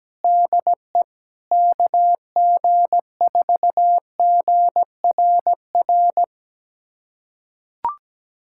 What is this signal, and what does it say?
Morse code, heard as a keyed tone, about 17 wpm, 700 Hz. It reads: DE KG4GRR
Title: USA Callsign DE Callsign - 6 Characters - 17wpm